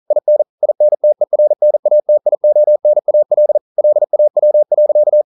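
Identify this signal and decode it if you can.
Morse code, heard as a keyed tone, about 42 wpm, 600 Hz. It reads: IN INTERNATIONAL LAW.
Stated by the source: Morse code practice at 41 wpm, from CW med Gnister.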